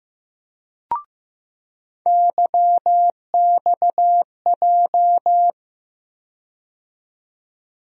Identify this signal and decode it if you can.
Morse code, heard as a keyed tone, about 15 wpm, 700 Hz. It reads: YXJ
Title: USA Call Sign Postfixes - 3 Letters 15wpm